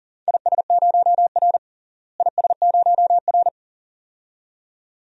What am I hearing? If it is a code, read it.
Morse code, heard as a keyed tone, about 40 wpm, 700 Hz. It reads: IS0R IS0R
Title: DXCC-100 Call Signs - 4 Characters - Repeated 2x - 40wpm